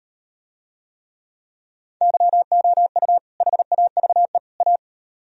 Morse code, heard as a keyed tone, about 38 wpm, 700 Hz. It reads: YOU HAVE A